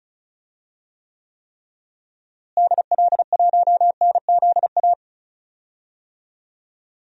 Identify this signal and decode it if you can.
Morse code, heard as a keyed tone, about 36 wpm, 700 Hz. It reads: DL1NZA
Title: DXCC-100 Call Signs - 6 Characters - Repeated 3x - 35wpm